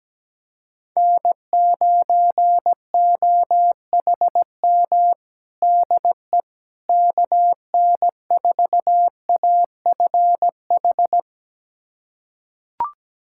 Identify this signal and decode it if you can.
Morse code, heard as a keyed tone, about 17 wpm, 700 Hz. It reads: N9OHM DE KN4AFH
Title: USA Callsign DE Callsign 17wpm